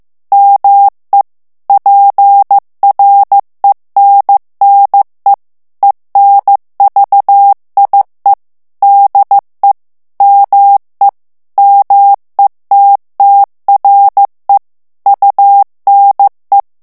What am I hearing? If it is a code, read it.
Morse code, heard as a keyed tone, about 15 wpm, 800 Hz. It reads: ME PRENNE ENVIE DE ME METTRE UNE
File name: morse mp3 / FRANCAIS TEXTE CLAIR 2 VITESSE 14 MOTS MINUTES